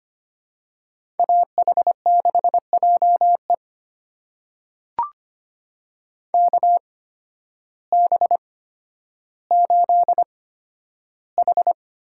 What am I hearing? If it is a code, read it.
Morse code, heard as a keyed tone, about 25 wpm, spaced as for 12 wpm, 700 Hz. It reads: AH6JE KB8H
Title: USA Call Signs - 8x Character Spacing - 25wpm